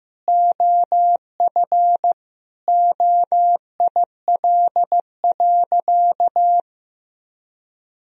Morse code, heard as a keyed tone, about 15 wpm, 700 Hz. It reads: OF OIL.
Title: Sentences from Top 1000 Words - Review Part 4 - 15wpm